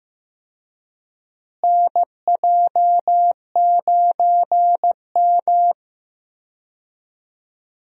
Morse code, heard as a keyed tone, about 15 wpm, 700 Hz. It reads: NJ9M
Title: USA Call Signs - 4 Characters - Repeated 3x - 15wpm